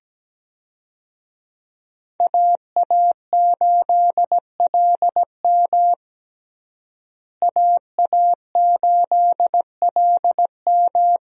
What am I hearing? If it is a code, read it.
Morse code, heard as a keyed tone, about 17 wpm, 700 Hz. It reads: AA8LM AA8LM
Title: USA Call Signs - 5 Characters - Repeated 2x - 17wpm